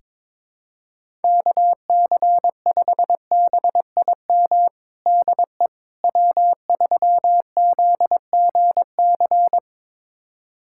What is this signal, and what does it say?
Morse code, heard as a keyed tone, about 22 wpm, 700 Hz. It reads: KC5BIM DE W3ZGC
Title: USA Callsign DE Callsign 22wpm